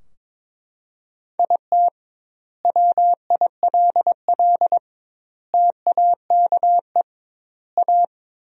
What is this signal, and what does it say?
Morse code, heard as a keyed tone, about 22 wpm, 700 Hz. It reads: IT WILL TAKE A
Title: 2x Word-Spacing - Sentences from Top 200 Words - 22wpm